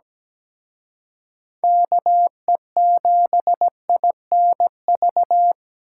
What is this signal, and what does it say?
Morse code, heard as a keyed tone, about 20 wpm, 700 Hz. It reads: KE7INV
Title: Speed Racing - USA 2x3 Call Signs 17wpm